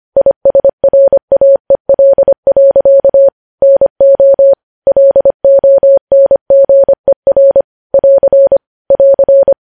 Morse code, heard as a keyed tone, about 25 wpm, 550 Hz. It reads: ISRAEL. NO LONGER <AR> <AR>
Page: news headlines in morse code at 25 wpm